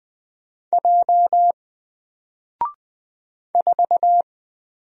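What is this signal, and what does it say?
Morse code, heard as a keyed tone, about 20 wpm, 700 Hz. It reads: J 4
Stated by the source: Single letter and number Morse drill at 20 wpm, from N4LCD.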